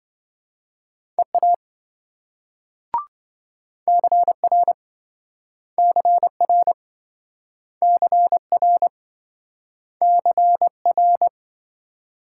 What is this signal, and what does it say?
Morse code, heard as a keyed tone, about 27 wpm, 700 Hz. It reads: EA CR CR CR CR